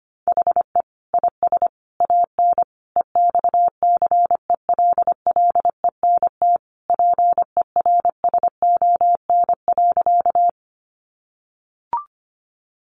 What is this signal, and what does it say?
Morse code, heard as a keyed tone, about 25 wpm, 700 Hz. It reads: HE IS AN EXCELLENT PERSON.
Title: Sentences from Top 1000 Words 25wpm